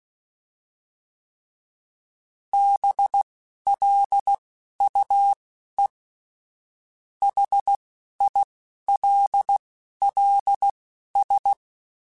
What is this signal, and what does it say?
Morse code, heard as a keyed tone, about 16 wpm, spaced as for 11 wpm, 800 Hz. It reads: BLUE HILLS